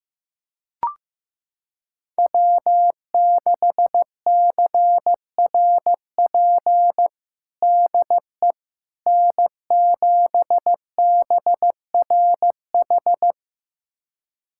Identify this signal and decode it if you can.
Morse code, heard as a keyed tone, about 16 wpm, 700 Hz. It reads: W6CRP DE N7BRH
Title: USA Callsign DE Callsign - 5 Characters - 15wpm